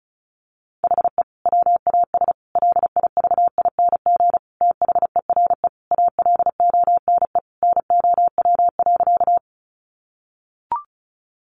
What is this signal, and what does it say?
Morse code, heard as a keyed tone, about 34 wpm, 700 Hz. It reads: HE WAS LIVING THERE ALONE NOW.
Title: Sentences from Top 1100 Words 35wpm